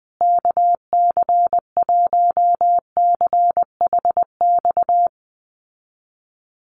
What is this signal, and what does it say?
Morse code, heard as a keyed tone, about 20 wpm, 700 Hz. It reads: KC1CHX
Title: USA Call Signs Repeated 3x 20wpm